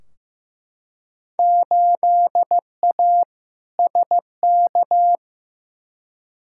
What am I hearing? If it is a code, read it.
Morse code, heard as a keyed tone, about 15 wpm, 700 Hz. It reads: 8A SK
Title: Field Day Exchanges Encore - 15wpm